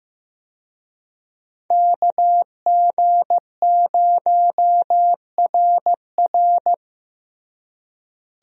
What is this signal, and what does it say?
Morse code, heard as a keyed tone, about 15 wpm, 700 Hz. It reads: KG0RR
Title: USA Call Signs - 5 Characters - Repeated 3x - 15wpm